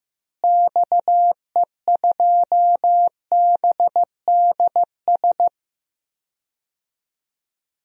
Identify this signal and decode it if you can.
Morse code, heard as a keyed tone, about 15 wpm, 700 Hz. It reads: XE2BDS
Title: DXCC-100 Call Signs - 6 Characters - Repeated 3x - 15wpm